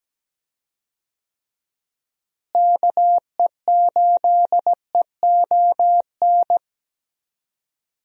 Morse code, heard as a keyed tone, about 17 wpm, 700 Hz. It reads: KE8EON
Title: USA Call Signs - 6 Characters - Repeated 2x - 17wpm